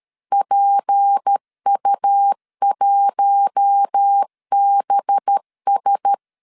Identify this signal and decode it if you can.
Morse code, heard as a keyed tone, about 13 wpm, 800 Hz. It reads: PU1BS
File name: LEZIONI / 6 PKX0 / 6 P 12wpm 2033